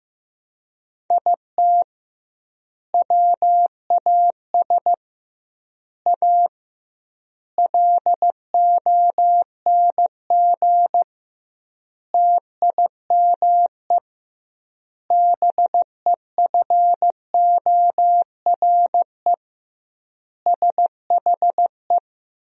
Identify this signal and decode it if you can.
Morse code, heard as a keyed tone, about 16 wpm, 700 Hz. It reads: IT WAS A LONG TIME BEFORE SHE
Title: 2x Word-Spacing - Sentences from Top 500 Words - 15wpm